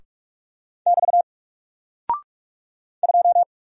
Morse code, heard as a keyed tone, about 44 wpm, 700 Hz. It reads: <BT> 2
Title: Single Character (Letter:Number:Punctuation:Prosign) Alt 45wpm